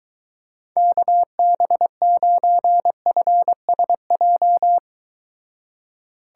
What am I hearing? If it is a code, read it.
Morse code, heard as a keyed tone, about 23 wpm, 700 Hz. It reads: KB9FSJ